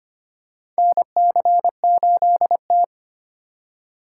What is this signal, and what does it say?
Morse code, heard as a keyed tone, about 25 wpm, 700 Hz. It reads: NC8T
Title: USA Call Signs - 4 Characters - 2x Character Spacing - 25wpm